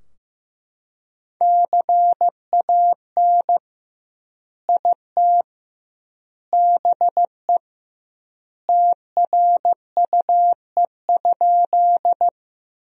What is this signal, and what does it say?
Morse code, heard as a keyed tone, about 15 wpm, 700 Hz. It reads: CAN IT BE TRUE?
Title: 2x Word-Spacing - Sentences from Top 400 Words - 15wpm